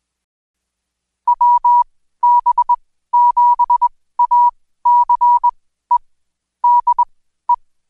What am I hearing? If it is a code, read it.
Morse code, heard as a keyed tone, about 21 wpm, spaced as for 14 wpm, 970 Hz. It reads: WB7ACE DE
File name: Code Quick / AudioDownld2 / Seg124-Day29-DayDrill 29